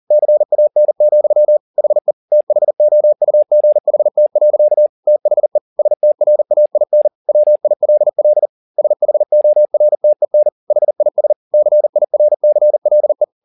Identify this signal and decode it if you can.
Morse code, heard as a keyed tone, about 40 wpm, 600 Hz. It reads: CAN, HE THOUGHT. THE STRAIN WILL SHORTEN HIS CIRCLE
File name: the old man and the sea 2 40wpm